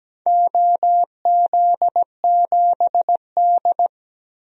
Morse code, heard as a keyed tone, about 17 wpm, 700 Hz. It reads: OZ7D
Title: DXCC-100 Call Signs - 4 Characters - 17wpm